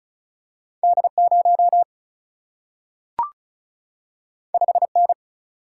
Morse code, heard as a keyed tone, about 36 wpm, 700 Hz. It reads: D0 5N
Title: Single-Letter Single-Number 35wpm